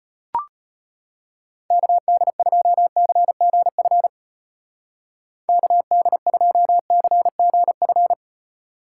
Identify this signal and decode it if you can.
Morse code, heard as a keyed tone, about 32 wpm, 700 Hz. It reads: KD2CGF KD2CGF